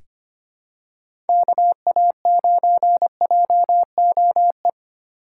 Morse code, heard as a keyed tone, about 25 wpm, 700 Hz. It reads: KA9JOE
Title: USA Call Signs - 6 Characters - 2x Character Spacing - 25wpm